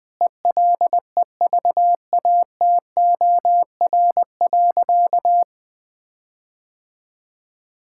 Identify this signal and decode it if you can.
Morse code, heard as a keyed tone, about 20 wpm, 700 Hz. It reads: ELEVATOR.
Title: Sentences from Top 1600 Words 20wpm